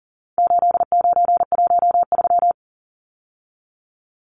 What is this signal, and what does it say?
Morse code, heard as a keyed tone, about 40 wpm, 700 Hz. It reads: 8913
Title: Numbers - 4-Digit 40wpm